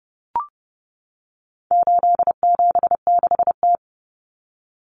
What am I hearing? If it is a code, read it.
Morse code, heard as a keyed tone, about 30 wpm, 700 Hz. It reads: 876T
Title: Common Cut Numbers - 4-Digit 30wpm